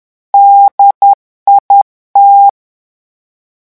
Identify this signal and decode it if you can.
Morse code, heard as a keyed tone, about 10 wpm, 800 Hz. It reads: DIT
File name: morse mp3 / revision francais 10 mots minute